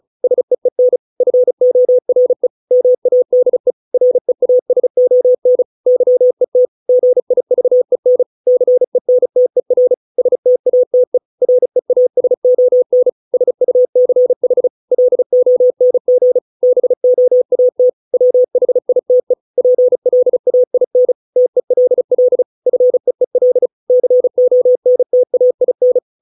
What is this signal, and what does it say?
Morse code, heard as a keyed tone, about 34 wpm, 500 Hz. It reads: SEEN FORE MADE REASON YET GIVEN CENTER STATE REASON SUCH LONG BOAT WHITE PLAIN TELL FEEL CONTAIN